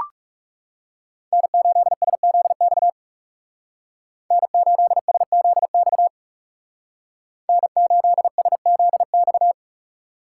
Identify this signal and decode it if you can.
Morse code, heard as a keyed tone, about 35 wpm, 700 Hz. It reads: N8SZX N8SZX N8SZX
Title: Speed Racing - USA 1x3 Call Signs 30wpm